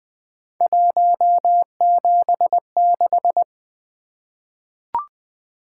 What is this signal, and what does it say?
Morse code, heard as a keyed tone, about 20 wpm, 700 Hz. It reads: 176
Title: Three Letter-Number Combinations - 20wpm